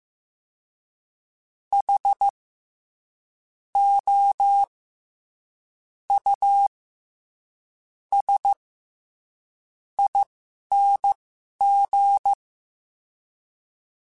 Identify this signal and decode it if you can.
Morse code, heard as a keyed tone, about 15 wpm, spaced as for 11 wpm, 800 Hz. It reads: H O U S ING
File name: Special INGs-mp3s / Special INGs-mp3s / SlowToFaster ing @10